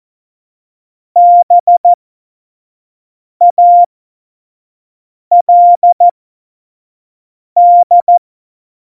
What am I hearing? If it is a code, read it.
Morse code, heard as a keyed tone, about 14 wpm, spaced as for 5 wpm, 700 Hz. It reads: BALD